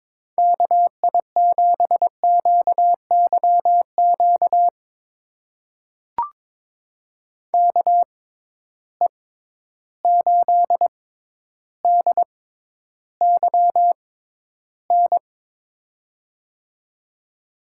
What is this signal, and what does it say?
Morse code, heard as a keyed tone, about 22 wpm, spaced as for 11 wpm, 700 Hz. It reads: KI7QYQ KE8DYN
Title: USA Call Signs - 6 Characters - 6x Character Spacing - 22wpm